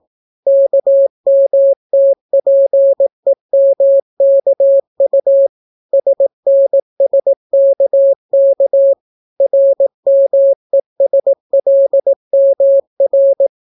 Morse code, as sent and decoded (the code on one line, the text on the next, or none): -.- -- - .--. -- -.- ..- / ... -. ... -.- -.- / .-. -- . ... .-.. -- .-.
KMTPEMKU SNSKK RMESLMR